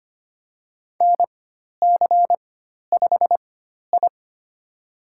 Morse code, heard as a keyed tone, about 25 wpm, spaced as for 12 wpm, 700 Hz. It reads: NC5I
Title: USA Call Signs - 4x Character Spacing - 25wpm